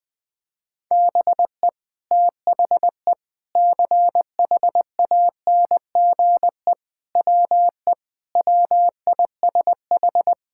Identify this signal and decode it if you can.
Morse code, heard as a keyed tone, about 20 wpm, 700 Hz. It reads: BE THE CHANGE WE WISH
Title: Sentences with 8-Word Phrases - 20wpm